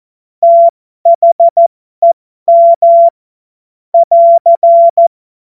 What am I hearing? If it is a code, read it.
Morse code, heard as a keyed tone, about 14 wpm, spaced as for 12 wpm, 680 Hz. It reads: THEM <AR>
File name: Morsekurs / Witze